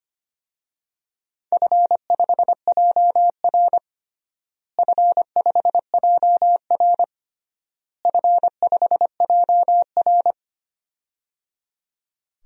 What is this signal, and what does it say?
Morse code, heard as a keyed tone, about 25 wpm, 700 Hz. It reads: F5JR F5JR F5JR
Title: DXCC-100 Call Signs - 4 Characters - Repeated 3x - 25wpm